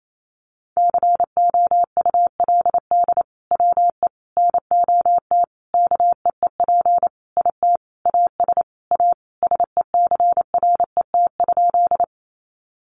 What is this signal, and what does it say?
Morse code, heard as a keyed tone, about 28 wpm, 700 Hz. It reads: COULD WE NOT KEEP IT AS A SECRET?
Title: Sentences from Top 1200 Words 28wpm